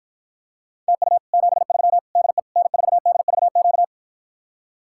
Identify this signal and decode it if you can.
Morse code, heard as a keyed tone, about 54 wpm, 700 Hz. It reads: TU 73 DE N4DVQ